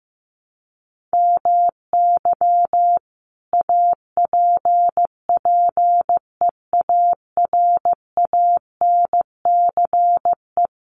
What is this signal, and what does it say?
Morse code, heard as a keyed tone, about 15 wpm, 700 Hz. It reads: MY APPEARANCE
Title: Sentences from Top 1100 Words 15wpm